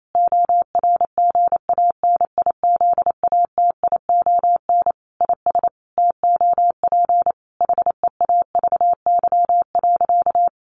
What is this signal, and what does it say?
Morse code, heard as a keyed tone, about 28 wpm, 700 Hz. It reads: ORGANIZATION IS TOP HEAVY.
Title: Sentences from Top 1100 Words 28wpm